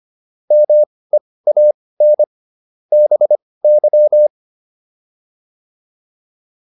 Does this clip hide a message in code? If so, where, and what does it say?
Yes: Morse code from 0.5 to 4.3 s, heard as a keyed tone, about 25 wpm, spaced as for 18 wpm, 600 Hz. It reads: MEAN BY